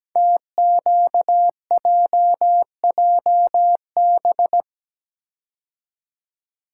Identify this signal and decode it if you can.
Morse code, heard as a keyed tone, about 17 wpm, 700 Hz. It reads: TQJJB
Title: Code Groups - 5 Letter:Numbers - 17wpm